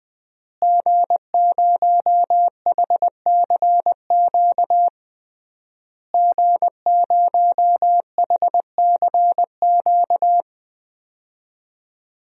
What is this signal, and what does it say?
Morse code, heard as a keyed tone, about 20 wpm, 700 Hz. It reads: G0HCQ G0HCQ
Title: DXCC-100 Call Signs - 5 Characters - Repeated 3x - 20wpm